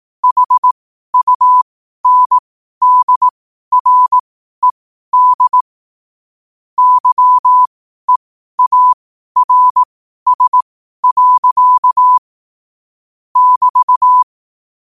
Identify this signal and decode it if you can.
Morse code, heard as a keyed tone, about 18 wpm, spaced as for 12 wpm, 1000 Hz. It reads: HUNDRED YEARS. <BT>
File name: qotdfiles / 12wpm